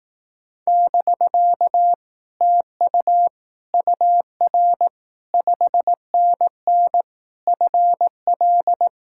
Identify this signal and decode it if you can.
Morse code, heard as a keyed tone, about 18 wpm, 700 Hz. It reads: <BK> TU UR 5NN FL
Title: Speed-Racing Common QSO Phrases Encore - 15wpm